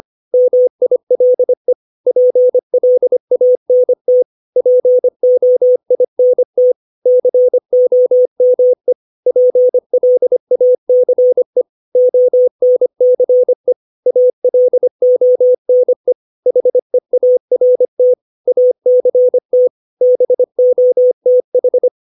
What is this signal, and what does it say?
Morse code, heard as a keyed tone, about 25 wpm, 500 Hz. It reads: MILE PLANT POINT COME PLACE ONCE ALONE HEART ACT BOTH
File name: 632 words / 632 Words 25WPM